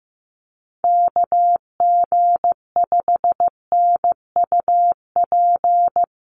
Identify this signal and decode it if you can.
Morse code, heard as a keyed tone, about 15 wpm, 700 Hz. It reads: KG5NUP
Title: USA Call Signs Repeated 2x 15wpm